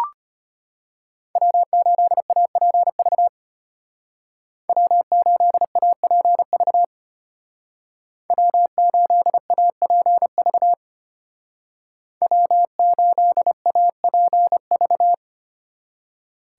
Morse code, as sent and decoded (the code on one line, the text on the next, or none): .-- ---.. .- .--. ...- / .-- ---.. .- .--. ...- / .-- ---.. .- .--. ...- / .-- ---.. .- .--. ...-
W8APV W8APV W8APV W8APV